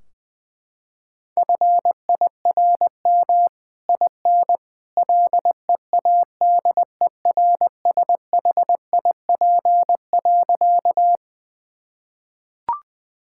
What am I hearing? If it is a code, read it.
Morse code, heard as a keyed tone, about 20 wpm, 700 Hz. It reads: FIRM IN LEADERSHIP.
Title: Sentences from Top 1600 Words 20wpm